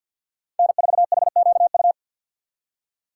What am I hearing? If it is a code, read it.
Morse code, heard as a keyed tone, about 52 wpm, 700 Hz. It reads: N4HQU
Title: USA Call Signs - 5 Characters - 3x Character Spacing - 50wpm